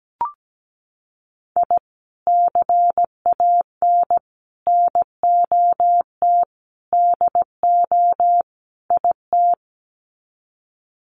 Morse code, heard as a keyed tone, about 17 wpm, 700 Hz. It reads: I CAN NOT DO IT
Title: Sets of 5 Words Encore 17wpm